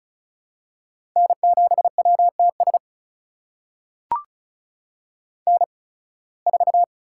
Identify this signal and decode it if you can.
Morse code, heard as a keyed tone, about 36 wpm, spaced as for 23 wpm, 700 Hz. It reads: N7WTS N4
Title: USA Call Signs - 5 Characters - 8x Character Spacing - 35wpm